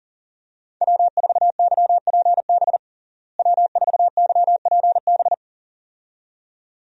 Morse code, heard as a keyed tone, about 40 wpm, 700 Hz. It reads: W4YPB W4YPB
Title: USA Call Signs - 5 Characters - Repeated 2x - 40wpm